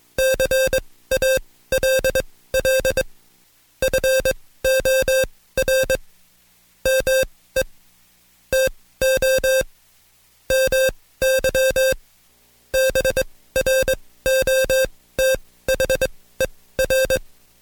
Morse code, heard as a keyed tone, about 22 wpm, spaced as for 16 wpm, 530 Hz. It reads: CALL FOR ME TO MY BROTHER
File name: Code Quick / AudioDownld2 / Seg134-Stepper7 15WPM